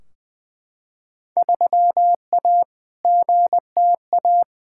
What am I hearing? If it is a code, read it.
Morse code, heard as a keyed tone, about 20 wpm, 700 Hz. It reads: 3A GTA
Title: Field Day Exchanges - 20wpm